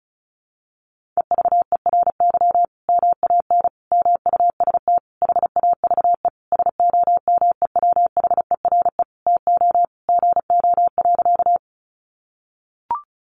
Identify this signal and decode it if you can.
Morse code, heard as a keyed tone, about 34 wpm, 700 Hz. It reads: EVERY MAN MUST HAVE SOMEWHERE TO GO.
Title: Sentences from Top 1400 Words 35wpm